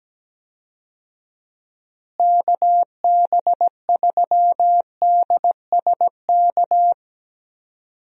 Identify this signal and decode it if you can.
Morse code, heard as a keyed tone, about 17 wpm, 700 Hz. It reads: KB3DSK
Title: USA Call Signs - 6 Characters - Repeated 2x - 17wpm